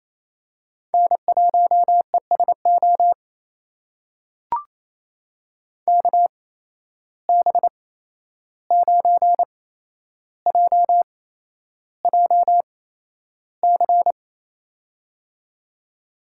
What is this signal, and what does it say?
Morse code, heard as a keyed tone, about 28 wpm, spaced as for 11 wpm, 700 Hz. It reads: N1ESO KB9JJC
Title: USA Call Signs - 8x Character Spacing - 28wpm